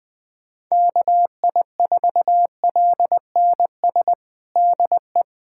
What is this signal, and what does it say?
Morse code, heard as a keyed tone, about 20 wpm, 700 Hz. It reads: KI4LNS DE
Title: USA Callsign DE Callsign 20wpm